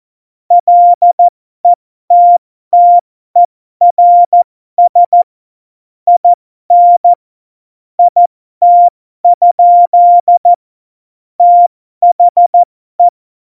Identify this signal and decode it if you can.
Morse code, heard as a keyed tone, about 14 wpm, spaced as for 12 wpm, 700 Hz. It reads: LETTERS IN IT? THE
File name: Morsekurs / Witze